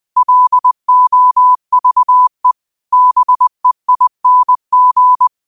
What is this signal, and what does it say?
Morse code, heard as a keyed tone, about 20 wpm, 1000 Hz. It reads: LOVE BEING
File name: qotdfiles / 20wpm